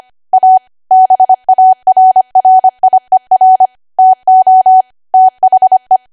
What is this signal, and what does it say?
Morse code, heard as a keyed tone, about 25 wpm, 750 Hz. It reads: A BARRIER TO THE
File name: Archive / 25 WPM / 25WPM